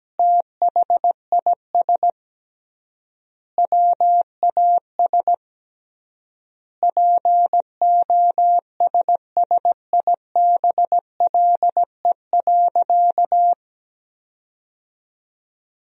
Morse code, heard as a keyed tone, about 17 wpm, 700 Hz. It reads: THIS WAS POSSIBLE.
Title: 3x Word-Spacing - Sentences from Top 500 Words - 17wpm